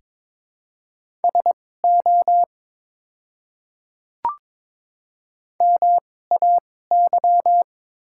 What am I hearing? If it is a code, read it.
Morse code, heard as a keyed tone, about 22 wpm, spaced as for 16 wpm, 700 Hz. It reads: SO MAY
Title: 2x Character Spacing - Top 200 Words - 22wpm